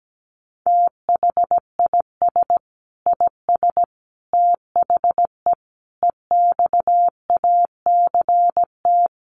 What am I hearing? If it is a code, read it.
Morse code, heard as a keyed tone, about 17 wpm, 700 Hz. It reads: THIS IS THE EXACT
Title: Sentences from Top 1300 Words 17wpm